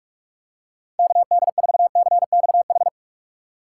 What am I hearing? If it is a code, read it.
Morse code, heard as a keyed tone, about 46 wpm, 700 Hz. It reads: KD4CXH
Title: USA Call Signs - 6 Characters - 2x Character Spacing - 45wpm